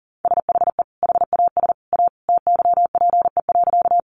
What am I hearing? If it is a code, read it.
Morse code, heard as a keyed tone, about 40 wpm, 700 Hz. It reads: SHE HAS A TYPE.
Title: Sentences from Top 600 Words 40wpm